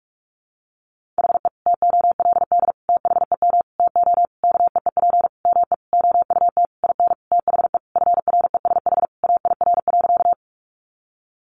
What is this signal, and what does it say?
Morse code, heard as a keyed tone, about 44 wpm, 700 Hz. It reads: HE TOLD THEM TO KEEP ME OUT IN THE FRESH AIR.
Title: Sentences from Top 1200 Words 45wpm